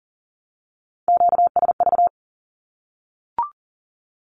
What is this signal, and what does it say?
Morse code, heard as a keyed tone, about 38 wpm, 700 Hz. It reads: QSV
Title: Q-Codes Repeated 3x 40wpm